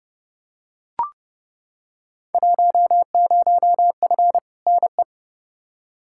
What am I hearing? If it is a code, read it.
Morse code, heard as a keyed tone, about 29 wpm, 700 Hz. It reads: 10F NE
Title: Field Day Exchanges Encore - 30wpm